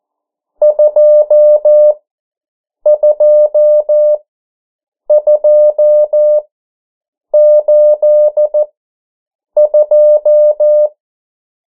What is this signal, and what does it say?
Morse code, heard as a keyed tone, about 14 wpm, spaced as for 7 wpm, 600 Hz. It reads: 22282
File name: morse mp3 / 09 urok 8